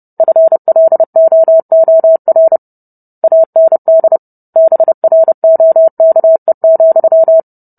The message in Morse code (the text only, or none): FLOOR AND BROKE,